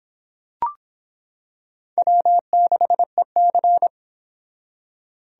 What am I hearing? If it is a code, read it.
Morse code, heard as a keyed tone, about 26 wpm, 700 Hz. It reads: W6EC